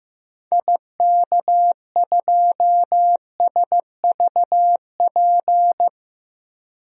Morse code, heard as a keyed tone, about 16 wpm, 700 Hz. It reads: IK2SVP
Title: DXCC-100 Call Signs - 6 Characters - 15wpm